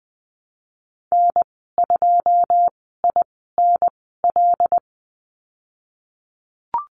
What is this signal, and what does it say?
Morse code, heard as a keyed tone, about 20 wpm, spaced as for 15 wpm, 700 Hz. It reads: N2INL
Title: 088 Lesson 15g - Number 5 - Call Signs 20wpm